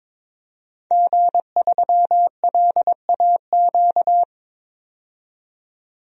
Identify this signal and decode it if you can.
Morse code, heard as a keyed tone, about 22 wpm, 700 Hz. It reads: G3LAQ
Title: DXCC-100 Call Signs - 5 Characters - 22wpm